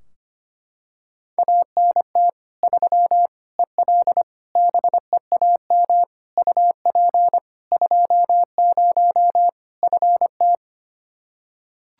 Morse code, heard as a keyed tone, about 25 wpm, 700 Hz. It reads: ANT 3 EL BEAM UP 20 FT
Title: Practice QSO - Exchange 2 - Info - 25wpm